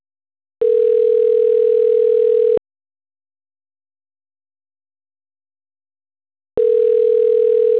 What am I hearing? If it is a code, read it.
Ringback tone (North America), 440 + 480 Hz.